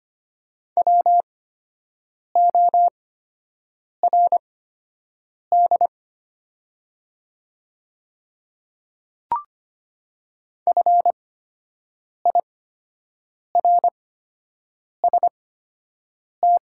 Morse code, heard as a keyed tone, about 25 wpm, spaced as for 7 wpm, 700 Hz. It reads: WORD FIRST
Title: 8x Character Spacing - Top 100 Words - 25wpm